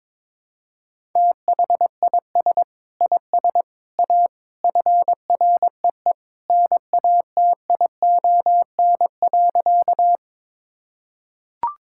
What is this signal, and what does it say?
Morse code, heard as a keyed tone, about 22 wpm, 700 Hz. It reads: THIS IS A FREE NATION.
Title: Sentences from Top 1000 Words - Review Part 6 - 22wpm